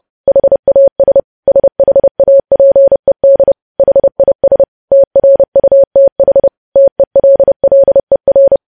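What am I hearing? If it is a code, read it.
Morse code, heard as a keyed tone, about 30 wpm, 550 Hz. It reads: HAS SHAPED HIS TRUTH TELLER
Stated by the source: Morse code practice of news headlines at 30 wpm, preceded by vvv.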